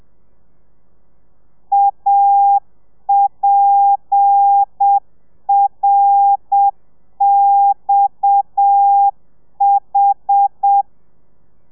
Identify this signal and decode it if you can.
Morse code, heard as a keyed tone, about 7 wpm, 800 Hz. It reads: APRXH
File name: telegrafia / i2tfj / Lez7 30